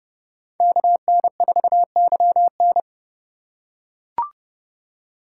Morse code, heard as a keyed tone, about 30 wpm, 700 Hz. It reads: KN4YN